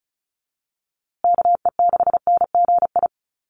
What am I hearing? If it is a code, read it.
Morse code, heard as a keyed tone, about 34 wpm, 700 Hz. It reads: KE6NGI
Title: USA Call Signs Repeated 2x 35wpm